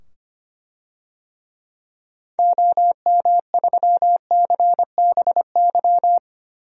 Morse code, heard as a keyed tone, about 25 wpm, 700 Hz. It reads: OM3CBY
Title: DXCC-100 Call Signs - 6 Characters - 25wpm